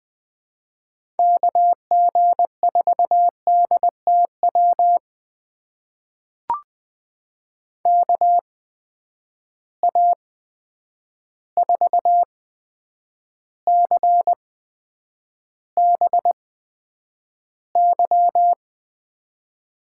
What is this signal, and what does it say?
Morse code, heard as a keyed tone, about 20 wpm, spaced as for 9 wpm, 700 Hz. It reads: KG4DTW KA4CBY
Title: USA Call Signs - 6 Characters - 8x Character Spacing - 20wpm